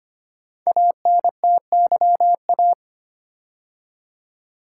Morse code, heard as a keyed tone, about 25 wpm, 700 Hz. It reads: ANTYA
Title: Code Groups - 5 Letter:Numbers - 25wpm